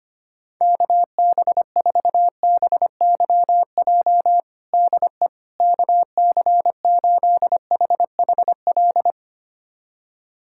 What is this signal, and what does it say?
Morse code, heard as a keyed tone, about 25 wpm, 700 Hz. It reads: KB4BYJ DE KC8HHL
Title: USA Callsign DE Callsign 25wpm